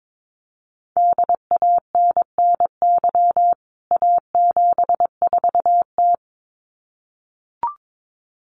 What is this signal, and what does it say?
Morse code, heard as a keyed tone, about 22 wpm, 700 Hz. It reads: DANNY A74T
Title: CWT Exchange 22wpm